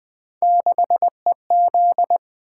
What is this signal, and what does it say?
Morse code, heard as a keyed tone, about 20 wpm, 700 Hz. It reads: 6EZ
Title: Three Letter-Number Combinations - 20wpm